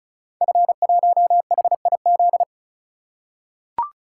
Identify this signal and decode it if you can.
Morse code, heard as a keyed tone, about 35 wpm, 700 Hz. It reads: F1HIZ
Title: Code Groups - 5 Letter:Numbers - 35wpm